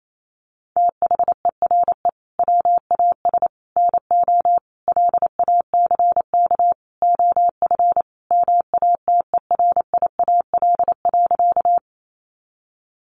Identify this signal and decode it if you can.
Morse code, heard as a keyed tone, about 28 wpm, 700 Hz. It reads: THERE WAS NO LACK OF MATERIAL.
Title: Sentences from Top 1100 Words 28wpm